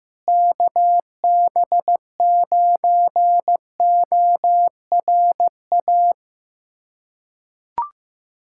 Morse code, heard as a keyed tone, about 15 wpm, 700 Hz. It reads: KB9ORA
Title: USA Callsign DE Callsign 15wpm